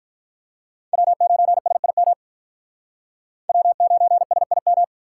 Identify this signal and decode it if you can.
Morse code, heard as a keyed tone, about 46 wpm, 700 Hz. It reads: W9SIM W9SIM